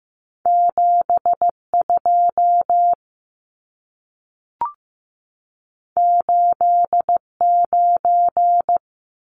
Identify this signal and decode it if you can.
Morse code, heard as a keyed tone, about 15 wpm, 700 Hz. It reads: 7 2 8 9
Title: Numbers - 2-Digit 15wpm